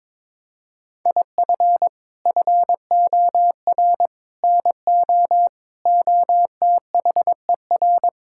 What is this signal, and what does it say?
Morse code, heard as a keyed tone, about 22 wpm, 700 Hz. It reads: IF FOR NO OTHER
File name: practice / 22wpm / Sets of 7 Words - 22wpm